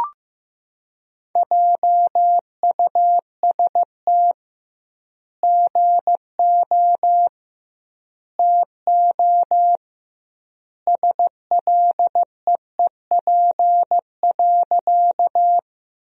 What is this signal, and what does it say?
Morse code, heard as a keyed tone, about 15 wpm, 700 Hz. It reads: JUST GO TO SLEEP.
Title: 2x Word-Spacing - Sentences from Top 500 Words - 15wpm